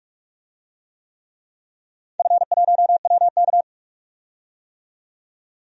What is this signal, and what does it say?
Morse code, heard as a keyed tone, about 44 wpm, 700 Hz. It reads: F1WK
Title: DXCC-100 Call Signs - 4 Characters - 45wpm